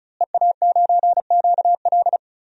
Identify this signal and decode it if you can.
Morse code, heard as a keyed tone, about 35 wpm, 700 Hz. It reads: EA9QL
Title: DXCC-100 Call Signs - 5 Characters - Repeated 3x - 35wpm